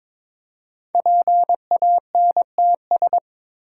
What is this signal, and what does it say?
Morse code, heard as a keyed tone, about 22 wpm, 700 Hz. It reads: PANTS
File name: 5 Letter Words - First 3 Letters Given - 22wpm